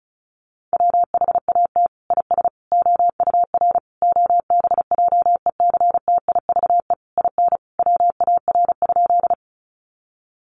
Morse code, heard as a keyed tone, about 34 wpm, 700 Hz. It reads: WHAT IS OUR OBJECTIVE IN WAR?
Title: Sentences from Top 1200 Words 35wpm